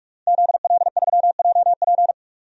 Morse code, heard as a keyed tone, about 46 wpm, 700 Hz. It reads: ZL3JP
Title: DXCC-100 Call Signs - 5 Characters - Repeated 2x - 45wpm